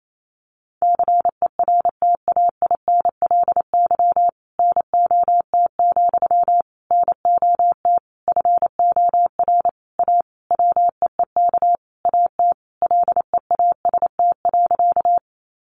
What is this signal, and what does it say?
Morse code, heard as a keyed tone, about 28 wpm, 700 Hz. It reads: CERTAINLY NOT, NOT FOR A WEEK AT LEAST.
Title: Sentences from Top 900 Words 28wpm